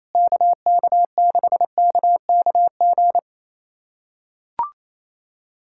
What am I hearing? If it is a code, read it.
Morse code, heard as a keyed tone, about 27 wpm, 700 Hz. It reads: KK6KKG
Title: USA Callsign DE Callsign - 6 Characters - 28wpm